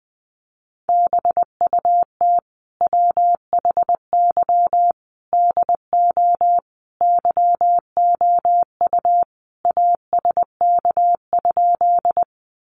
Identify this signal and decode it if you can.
Morse code, heard as a keyed tone, about 20 wpm, 700 Hz. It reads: BUT WHY DO YOU ASK?
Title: Sentences from Top 200 Words 20wpm